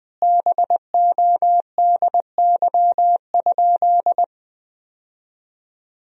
Morse code, heard as a keyed tone, about 20 wpm, 700 Hz. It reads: BODY?
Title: 3x Word-Spacing - Sentences from Top 400 Words - 20wpm